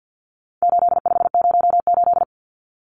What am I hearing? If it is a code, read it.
Morse code, heard as a keyed tone, about 50 wpm, 700 Hz. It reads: 8508
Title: Numbers - 4-Digit 50wpm